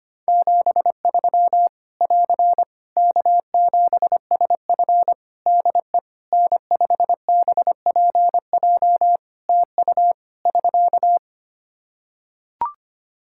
Morse code, heard as a keyed tone, about 25 wpm, 700 Hz. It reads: 73 <AR> K7SF DE N5BPJ TU <SK>
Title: Practice QSO - Exchange 4 - Ending - 25wpm